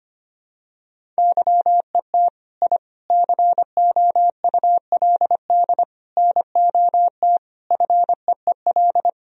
Morse code, heard as a keyed tone, about 25 wpm, 700 Hz. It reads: YET I COULD NOT FEEL
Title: Sentences from Top 1000 Words - Review Part 1 - 25wpm